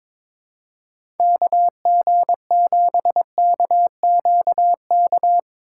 Morse code, heard as a keyed tone, about 22 wpm, 700 Hz. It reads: KG7KQK
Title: USA Call Signs - 6 Characters - 3x Character Spacing - 22wpm